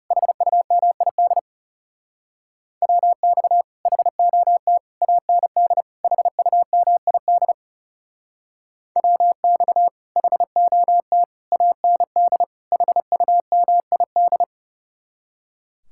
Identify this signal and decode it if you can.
Morse code, heard as a keyed tone, about 38 wpm, 700 Hz. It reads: HUMID WX HOT AND HUMID WX HOT AND HUMID
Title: Speed-Racing Common QSO Phrases - 30wpm